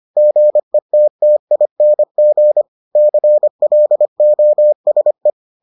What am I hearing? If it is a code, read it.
Morse code, heard as a keyed tone, about 25 wpm, 600 Hz. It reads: GETTING CLOSE